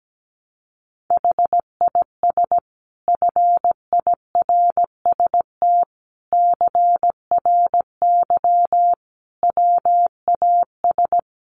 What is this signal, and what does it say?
Morse code, heard as a keyed tone, about 17 wpm, 700 Hz. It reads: HIS FIRST CRY WAS
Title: Sentences from Top 500 Words 17wpm